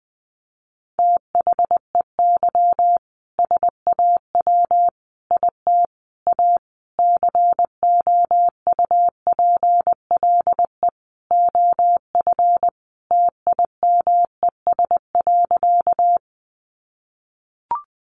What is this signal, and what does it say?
Morse code, heard as a keyed tone, about 20 wpm, 700 Hz. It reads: THEY SAW IT A COUPLE OF TIMES.
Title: Sentences from Top 1100 Words 20wpm